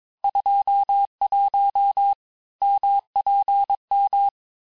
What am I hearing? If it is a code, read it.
Morse code, heard as a keyed tone, about 22 wpm, 800 Hz. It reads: 21 MPM
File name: texte clair vitesse 16 puis 21 mpm